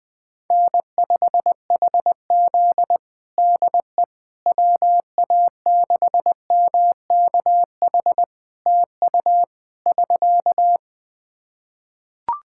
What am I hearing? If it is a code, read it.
Morse code, heard as a keyed tone, about 20 wpm, 700 Hz. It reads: N5HZ DE WA6MKH TU <SK>
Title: Practice QSO - Exchange 4 - Ending - 20wpm